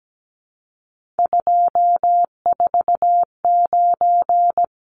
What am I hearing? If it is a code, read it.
Morse code, heard as a keyed tone, about 17 wpm, 700 Hz. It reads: 249
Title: Numbers - 3-Digit 17wpm